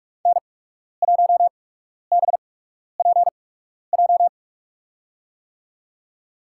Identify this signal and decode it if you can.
Morse code, heard as a keyed tone, about 44 wpm, spaced as for 12 wpm, 690 Hz. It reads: N1BPJ
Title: USA Call Signs - 5 Characters - 8x Character Spacing - 45wpm